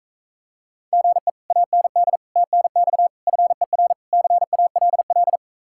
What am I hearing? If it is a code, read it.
Morse code, heard as a keyed tone, about 35 wpm, 700 Hz. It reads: GE AND TNX FER CALL